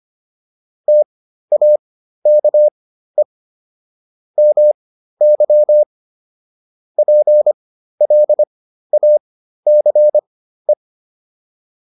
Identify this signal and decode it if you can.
Morse code, heard as a keyed tone, about 25 wpm, spaced as for 13 wpm, 600 Hz. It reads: TAKE MY PLACE